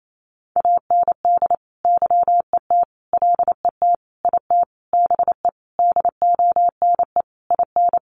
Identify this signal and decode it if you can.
Morse code, heard as a keyed tone, about 28 wpm, 700 Hz. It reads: AND YET LET IT BE DONE IN
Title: Sentences from Top 1200 Words 28wpm